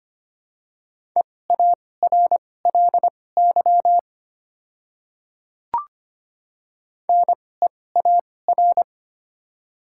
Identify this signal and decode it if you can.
Morse code, heard as a keyed tone, about 25 wpm, spaced as for 18 wpm, 700 Hz. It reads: EARLY NEAR
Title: 2x Character Spacing - Top 500 Words - 25wpm